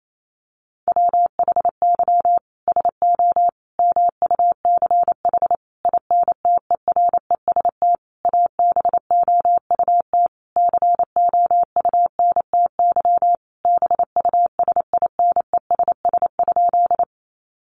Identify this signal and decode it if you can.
Morse code, heard as a keyed tone, about 28 wpm, 700 Hz. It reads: WHY SO MUCH INTEREST ABOUT COUNTY BUSINESS?
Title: Sentences from Top 1200 Words 28wpm